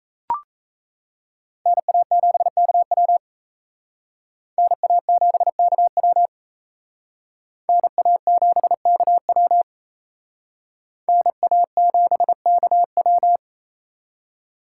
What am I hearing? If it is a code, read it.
Morse code, heard as a keyed tone, about 36 wpm, 700 Hz. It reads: NA7KW NA7KW NA7KW NA7KW